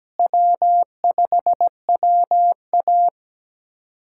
Morse code, heard as a keyed tone, about 17 wpm, 700 Hz. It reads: W5WA
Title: USA Callsign DE Callsign - 4 Characters - 17wpm